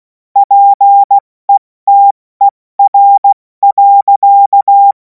Morse code, heard as a keyed tone, about 16 wpm, 800 Hz. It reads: PETER.
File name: Morsekurs / Witze / Witz 07. April 2026 70 F80